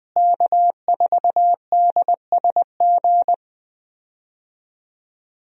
Morse code, heard as a keyed tone, about 20 wpm, 700 Hz. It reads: K4DSG
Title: USA Call Signs - 5 Characters - Repeated 2x - 20wpm